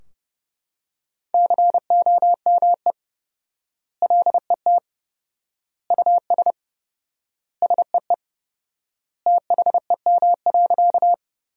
Morse code, heard as a keyed tone, about 32 wpm, 700 Hz. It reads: COME LET US SEE THEM.
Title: 4x Word-Spacing - Sentences from Top 300 Words - 30wpm